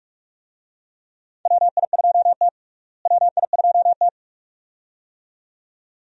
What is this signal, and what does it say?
Morse code, heard as a keyed tone, about 46 wpm, 700 Hz. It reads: WI2T WI2T
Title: USA Call Signs - 4 Characters - Repeated 2x - 45wpm